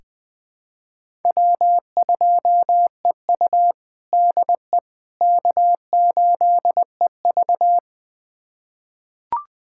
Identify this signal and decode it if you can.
Morse code, heard as a keyed tone, about 20 wpm, 700 Hz. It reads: W2EU DE K8EV
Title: USA Callsign DE Callsign - 4 Characters - 20wpm